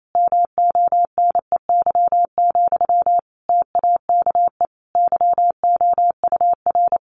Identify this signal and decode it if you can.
Morse code, heard as a keyed tone, about 28 wpm, 700 Hz. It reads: MONEY, TAKE YOUR
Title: Sentences from Top 900 Words 28wpm